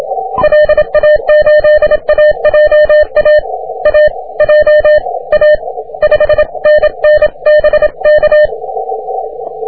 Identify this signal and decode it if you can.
Morse code, heard as a keyed tone, about 27 wpm, spaced as for 23 wpm, 1800 Hz. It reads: LA8AJA AJA 5NN BK
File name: wp / s04r 30m cw